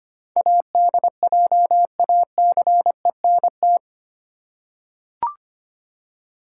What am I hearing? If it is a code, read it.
Morse code, heard as a keyed tone, about 25 wpm, 700 Hz. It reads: ADJACENT
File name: Eight Letter Words - 25wpm